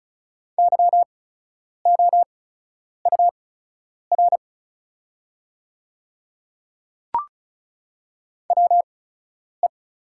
Morse code, heard as a keyed tone, about 36 wpm, spaced as for 10 wpm, 700 Hz. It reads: YOUR WE
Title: 8x Character Spacing - Top 100 Words - 35wpm